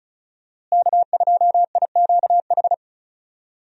Morse code, heard as a keyed tone, about 36 wpm, 700 Hz. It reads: K2IQH
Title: USA Call Signs - 5 Characters - Repeated 3x - 35wpm